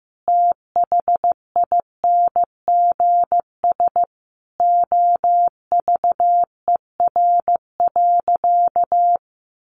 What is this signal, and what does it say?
Morse code, heard as a keyed tone, about 15 wpm, 700 Hz. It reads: THINGS OVER.
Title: Sentences from Top 1400 Words 15wpm